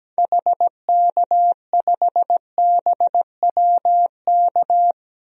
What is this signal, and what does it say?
Morse code, heard as a keyed tone, about 17 wpm, 700 Hz. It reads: HK5BWK
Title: DXCC-100 Call Signs - 6 Characters - Repeated 3x - 17wpm